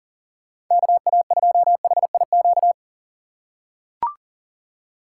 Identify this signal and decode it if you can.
Morse code, heard as a keyed tone, about 44 wpm, 700 Hz. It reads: KA2HIQ
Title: USA Call Signs - 6 Characters - 6x Character Spacing - 40wpm